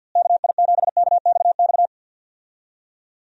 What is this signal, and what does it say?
Morse code, heard as a keyed tone, about 53 wpm, 700 Hz. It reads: KI7KXX
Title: USA Call Signs - 6 Characters - Repeated 3x - 50wpm